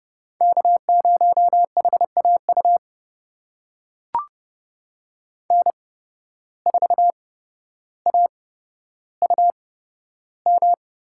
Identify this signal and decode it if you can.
Morse code, heard as a keyed tone, about 30 wpm, spaced as for 13 wpm, 700 Hz. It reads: K0HAU N4AUM
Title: USA Call Signs - 5 Characters - 8x Character Spacing - 30wpm